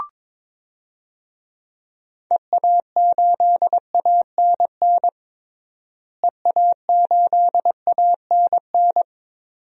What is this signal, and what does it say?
Morse code, heard as a keyed tone, about 22 wpm, 700 Hz. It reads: EA8ANN EA8ANN
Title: DXCC-100 Call Signs - 6 Characters - Repeated 2x - 22wpm